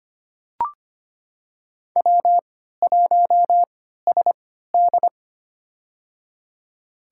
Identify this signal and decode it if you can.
Morse code, heard as a keyed tone, about 25 wpm, spaced as for 14 wpm, 700 Hz. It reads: W1SD
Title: USA Call Signs - 4 Characters - 3x Character Spacing - 25wpm